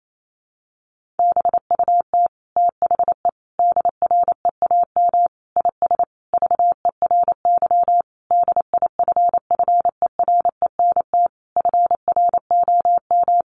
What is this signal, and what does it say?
Morse code, heard as a keyed tone, about 28 wpm, 700 Hz. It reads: BUT THE DREAM IS VERY DIFFERENT FROM
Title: Sentences from Top 1500 Words 28wpm